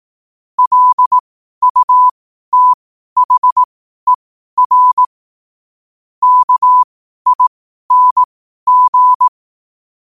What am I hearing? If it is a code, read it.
Morse code, heard as a keyed tone, about 18 wpm, spaced as for 12 wpm, 1000 Hz. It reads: LUTHER KING
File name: qotdfiles / 12wpm